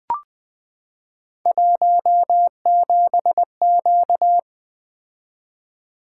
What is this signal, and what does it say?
Morse code, heard as a keyed tone, about 20 wpm, 700 Hz. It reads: 17Q
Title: Three Letter-Number Combinations - 20wpm